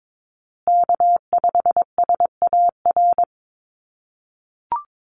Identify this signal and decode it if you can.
Morse code, heard as a keyed tone, about 22 wpm, 700 Hz. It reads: K5SAR
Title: USA Call Signs Repeated 3x 22wpm